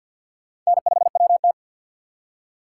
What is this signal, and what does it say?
Morse code, heard as a keyed tone, about 52 wpm, 700 Hz. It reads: N5WT
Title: USA Call Signs - 4 Characters - 3x Character Spacing - 50wpm